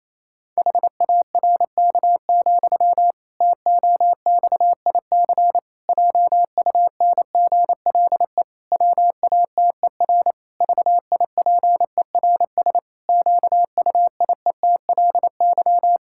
Morse code, heard as a keyed tone, about 27 wpm, 700 Hz. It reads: HARK, TOXIC JUNGLE WATER VIPERS QUIETLY
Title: Pangram Challenge v2 - 28wpm